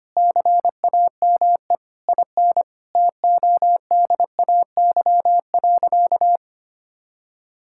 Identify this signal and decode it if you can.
Morse code, heard as a keyed tone, about 25 wpm, 700 Hz. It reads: CAME IN TODAY.
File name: practice / 25wpm / Sentences with 4-Word Phrases - 25wpm